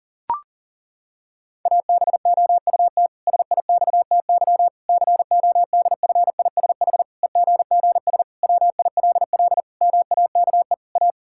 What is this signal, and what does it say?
Morse code, heard as a keyed tone, about 37 wpm, 700 Hz. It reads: ABOUT SIXTY CODFISH EGGS WILL MAKE A
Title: Pangram Challenge v2 - 40wpm